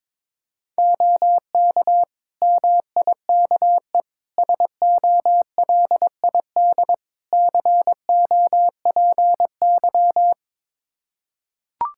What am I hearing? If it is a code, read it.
Morse code, heard as a keyed tone, about 22 wpm, 700 Hz. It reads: OK MIKE SOLID COPY
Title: Practice QSO - Exchange 3 - More Info - 22wpm